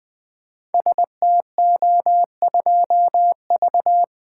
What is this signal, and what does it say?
Morse code, heard as a keyed tone, about 20 wpm, 700 Hz. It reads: STO2V
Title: Code Groups - 5 Letter:Numbers - 20wpm